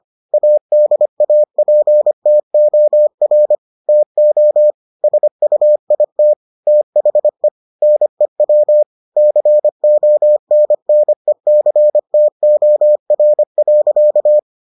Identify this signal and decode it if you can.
Morse code, heard as a keyed tone, about 25 wpm, 600 Hz. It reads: ADAPTOR TO SUIT THE NEW CONNECTOR.